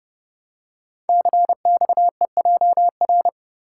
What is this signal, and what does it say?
Morse code, heard as a keyed tone, about 30 wpm, 700 Hz. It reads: CXEJR
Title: Code Groups - 5 Letter:Numbers - 30wpm